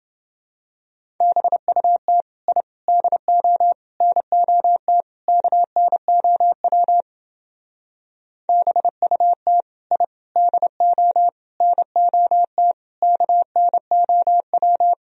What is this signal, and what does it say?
Morse code, heard as a keyed tone, about 29 wpm, 700 Hz. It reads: BUT I DO NOT KNOW BUT I DO NOT KNOW